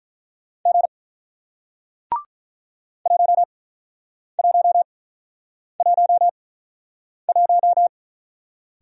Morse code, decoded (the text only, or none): G 1111